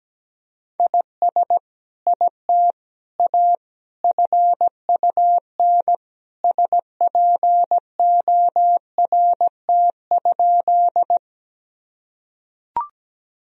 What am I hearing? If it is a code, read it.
Morse code, heard as a keyed tone, about 17 wpm, 700 Hz. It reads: IS IT A FUN SPORT?
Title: Sentences from Top 1000 Words - Review Part 4 - 17wpm